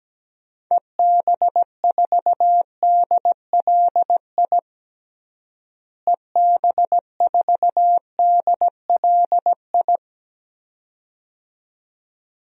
Morse code, heard as a keyed tone, about 17 wpm, 700 Hz. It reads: EB4DLI EB4DLI
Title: DXCC-100 Call Signs - 6 Characters - Repeated 2x - 17wpm